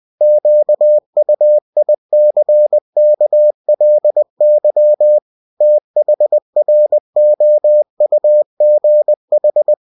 Morse code, heard as a keyed tone, about 20 wpm, 600 Hz. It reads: QUICKLY THROUGH
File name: cw-books / winnie-the-pooh / winnie-20wpm-page-0004